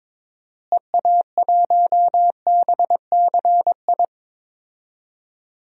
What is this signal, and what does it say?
Morse code, heard as a keyed tone, about 21 wpm, 700 Hz. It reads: EA1BCI
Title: DXCC-100 Call Signs - 6 Characters - Repeated 3x - 22wpm